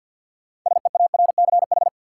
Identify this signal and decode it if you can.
Morse code, heard as a keyed tone, about 53 wpm, 700 Hz. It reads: SEARCH
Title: Sentences from Top 800 Words - New Word Drill - 50wpm